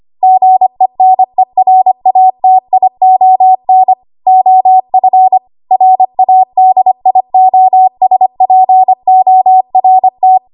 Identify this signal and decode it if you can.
Morse code, heard as a keyed tone, about 25 wpm, 750 Hz. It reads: GENERATION OF RADIOSPORT
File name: Archive / 25 WPM / 25WPM